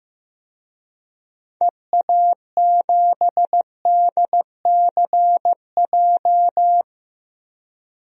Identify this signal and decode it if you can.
Morse code, heard as a keyed tone, about 15 wpm, 700 Hz. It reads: EA7DCJ
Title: DXCC-100 Call Signs - 6 Characters - Repeated 2x - 15wpm